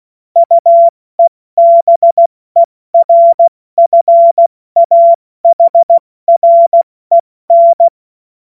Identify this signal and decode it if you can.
Morse code, heard as a keyed tone, about 16 wpm, 680 Hz. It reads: UEBERFAHREN
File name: Morsekurs / Witze / Witz 12. Oktober 2021, 70 F80